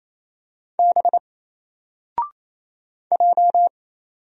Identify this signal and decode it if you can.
Morse code, heard as a keyed tone, about 27 wpm, 700 Hz. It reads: B J
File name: Single-Letter-Number-28wpm